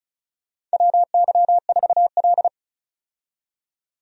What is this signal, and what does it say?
Morse code, heard as a keyed tone, about 36 wpm, 700 Hz. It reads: WY4L
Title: USA Call Signs - 4 Characters - 35wpm